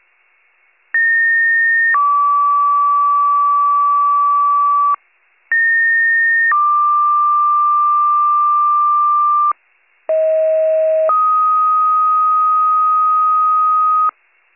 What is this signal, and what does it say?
Two-tone page, heard as a one-second tone, then a three-second tone, about 1850 then 1150 Hz, about 1850 then 1180 Hz, about 640 then 1250 Hz. It is heard from 0.9 to 14.1 s.